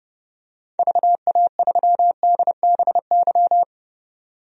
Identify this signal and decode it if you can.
Morse code, heard as a keyed tone, about 30 wpm, 700 Hz. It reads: VA3DBY
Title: DXCC-100 Call Signs - 6 Characters - 30wpm